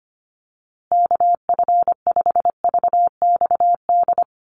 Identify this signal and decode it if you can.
Morse code, heard as a keyed tone, about 25 wpm, 700 Hz. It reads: KF5VXD
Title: USA Call Signs Repeated 3x 25wpm